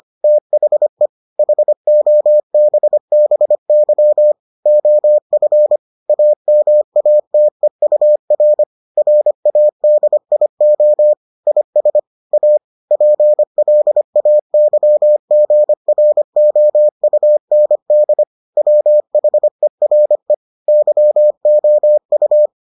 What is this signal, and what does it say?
Morse code, heard as a keyed tone, about 25 wpm, 600 Hz. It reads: THE HOBBY OF AMATEUR RADIO IS A PLAYGROUND WHERE YOU